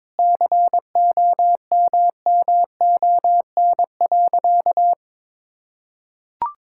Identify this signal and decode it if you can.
Morse code, heard as a keyed tone, about 22 wpm, 700 Hz. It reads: COMMON.
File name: practice / 22wpm / Sentences with 4-Word Phrases - 22wpm